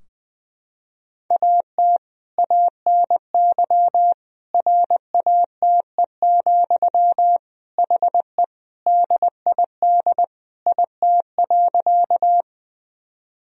Morse code, heard as a keyed tone, about 21 wpm, 700 Hz. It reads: AT ANY RATE, HE DID IT.
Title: Sentences from Top 1000 Words - Review Part 4 - 20wpm